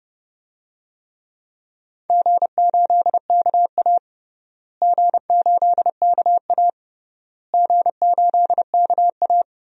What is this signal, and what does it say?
Morse code, heard as a keyed tone, about 30 wpm, 700 Hz. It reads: G8KA G8KA G8KA
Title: DXCC-100 Call Signs - 4 Characters - Repeated 3x - 30wpm